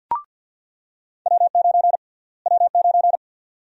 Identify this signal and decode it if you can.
Morse code, heard as a keyed tone, about 49 wpm, 700 Hz. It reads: W9 W9
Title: USA Call Sign Prefixes - Repeated 2x - 50wpm